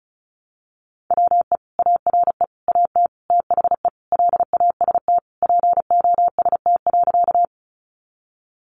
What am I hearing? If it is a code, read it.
Morse code, heard as a keyed tone, about 34 wpm, 700 Hz. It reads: WE ARE AT THE LAST POST.
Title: Sentences from Top 1000 Words 35wpm